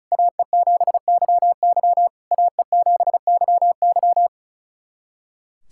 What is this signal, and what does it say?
Morse code, heard as a keyed tone, about 34 wpm, 700 Hz. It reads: AE7YY AE7YY
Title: SOTA QSO Practice 35wpm